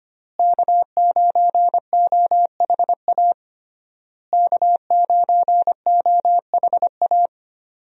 Morse code, heard as a keyed tone, about 25 wpm, 700 Hz. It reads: K9OHA K9OHA
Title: USA Call Signs - 5 Characters - Repeated 3x - 25wpm